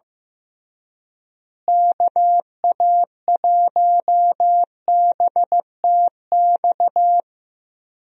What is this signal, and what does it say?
Morse code, heard as a keyed tone, about 15 wpm, 700 Hz. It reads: KA1BTX
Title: USA Call Signs - 6 Characters - Repeated 2x - 15wpm